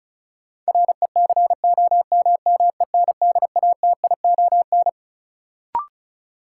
Morse code, heard as a keyed tone, about 36 wpm, 700 Hz. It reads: RECOMMENDATION
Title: Sentences from Top 1400 Words - New Word Drill - 35wpm